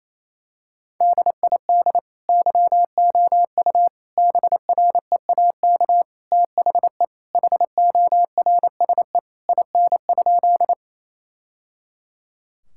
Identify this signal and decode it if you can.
Morse code, heard as a keyed tone, about 28 wpm, 700 Hz. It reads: DID YOU BREAK THE HORSE IN?
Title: Sentences from Top 1000 Words - Review Part 6 - 28wpm